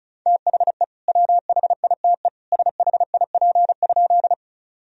Morse code, heard as a keyed tone, about 36 wpm, 700 Hz. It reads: THE WHITE SHIP?
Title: Sentences from Top 1000 Words - Review Part 2 - 35wpm